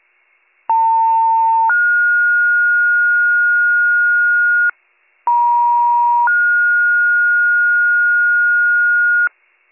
Two-tone page, heard as a one-second tone, then a three-second tone, about 900 then 1430 Hz, about 950 then 1430 Hz. It is heard from 0.7 to 9.3 s.